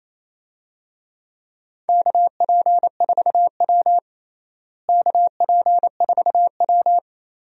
Morse code, heard as a keyed tone, about 28 wpm, 700 Hz. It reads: KP4W KP4W
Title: DXCC-100 Call Signs - 4 Characters - Repeated 3x - 28wpm